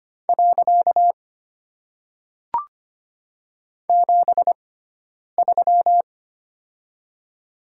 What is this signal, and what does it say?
Morse code, heard as a keyed tone, about 25 wpm, 700 Hz. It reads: . 7 3
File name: Top 90 QSO Elements - 6x Character Spacing - 25wpm